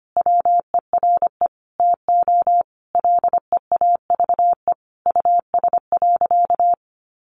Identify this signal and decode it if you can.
Morse code, heard as a keyed tone, about 25 wpm, 700 Hz. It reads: WERE TO LEAVE US.
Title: Sentences from Top 400 Words 25wpm